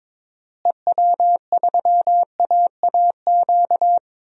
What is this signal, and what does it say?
Morse code, heard as a keyed tone, about 22 wpm, 700 Hz. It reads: EW3AAQ
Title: DXCC-100 Call Signs - 6 Characters - Repeated 2x - 22wpm